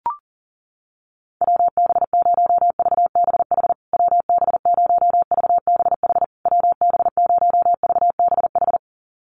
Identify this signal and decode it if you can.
Morse code, heard as a keyed tone, about 40 wpm, 700 Hz. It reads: WB0VBH WB0VBH WB0VBH
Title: USA Call Signs Repeated 3x 40wpm